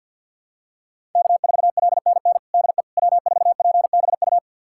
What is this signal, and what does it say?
Morse code, heard as a keyed tone, about 50 wpm, 700 Hz. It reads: K4LNN DE W4PBU
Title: USA Callsign DE Callsign - 5 Characters - 50wpm